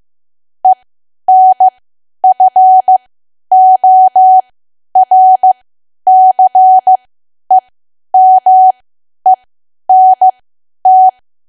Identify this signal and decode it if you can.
Morse code, heard as a keyed tone, about 15 wpm, spaced as for 10 wpm, 750 Hz. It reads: ENFORCEMENT
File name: cwa / W1AW-Level2 / 10WPM